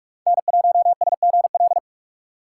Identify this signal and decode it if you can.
Morse code, heard as a keyed tone, about 46 wpm, 700 Hz. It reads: N1SGL
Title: USA Call Signs - 5 Characters - 3x Character Spacing - 45wpm